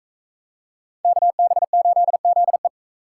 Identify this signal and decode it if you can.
Morse code, heard as a keyed tone, about 41 wpm, 700 Hz. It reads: KB8ZE